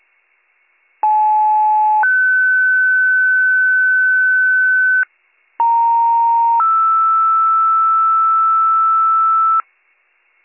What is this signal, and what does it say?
Two-tone page, heard as a one-second tone, then a three-second tone, about 860 then 1500 Hz, about 930 then 1320 Hz. It is heard from 1.0 to 9.6 s.